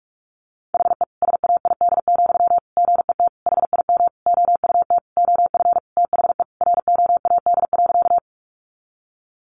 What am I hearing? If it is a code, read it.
Morse code, heard as a keyed tone, about 44 wpm, 700 Hz. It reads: HE SAID, GET HIM OUT OF THE ROAD.
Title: Sentences from Top 400 Words 45wpm